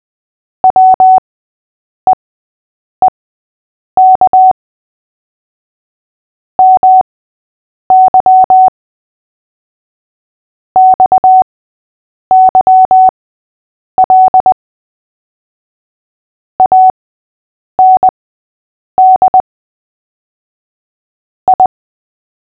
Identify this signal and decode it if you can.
Morse code, heard as a keyed tone, about 20 wpm, spaced as for 8 wpm, 750 Hz. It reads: WEEK MY XYL AND I
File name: cw / fists / Short-Story-5-8wpm